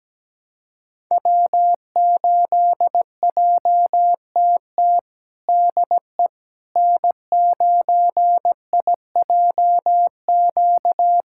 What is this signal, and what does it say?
Morse code, heard as a keyed tone, about 17 wpm, 700 Hz. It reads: W8JTT DE N9IJQ
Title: USA Callsign DE Callsign - 5 Characters - 17wpm